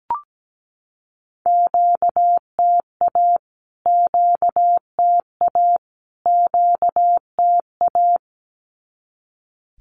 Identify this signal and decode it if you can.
Morse code, heard as a keyed tone, about 17 wpm, 700 Hz. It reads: QTA QTA QTA
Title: Q-Codes Repeated 3x 17wpm